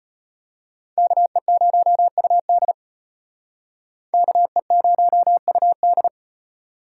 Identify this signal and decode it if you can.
Morse code, heard as a keyed tone, about 36 wpm, 700 Hz. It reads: KE0UD KE0UD